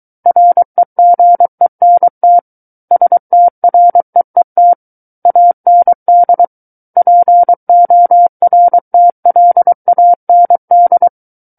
Morse code, heard as a keyed tone, about 23 wpm, 700 Hz. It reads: REGENT STREET AND PORTLAND
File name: Book One / 14-Chapter 14 - 23 WPM